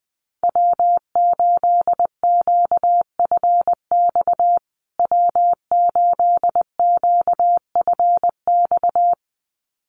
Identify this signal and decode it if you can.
Morse code, heard as a keyed tone, about 20 wpm, 700 Hz. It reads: W8QFX W8QFX
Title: USA Call Signs Repeated 2x 20wpm